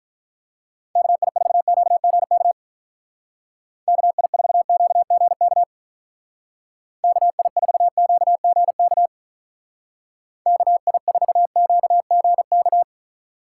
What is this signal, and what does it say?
Morse code, heard as a keyed tone, about 42 wpm, 700 Hz. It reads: KI4QGK KI4QGK KI4QGK KI4QGK
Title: Speed Racing - USA 2x3 Call Signs 35wpm